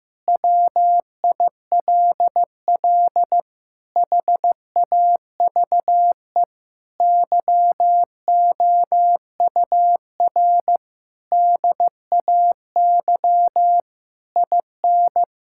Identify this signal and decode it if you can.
Morse code, heard as a keyed tone, about 15 wpm, 700 Hz. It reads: WILL HAVE YOUR DAY IN
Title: Sentences from Top 1600 Words 15wpm